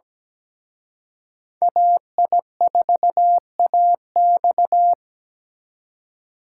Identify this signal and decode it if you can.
Morse code, heard as a keyed tone, about 20 wpm, 700 Hz. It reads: AI4AX
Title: Speed Racing - USA 2x2 Call Signs 17wpm